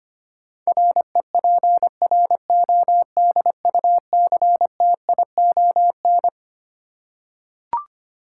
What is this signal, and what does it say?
Morse code, heard as a keyed tone, about 25 wpm, 700 Hz. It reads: REPRODUCTION
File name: Twelve Letter Words - 25wpm